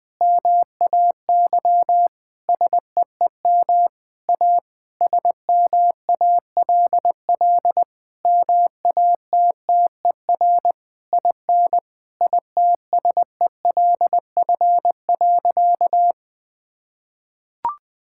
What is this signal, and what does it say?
Morse code, heard as a keyed tone, about 20 wpm, 700 Hz. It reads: MAY SEEM A SMALL MATTER IN ITSELF.
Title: Sentences from Top 1000 Words - Review Part 5 - 20wpm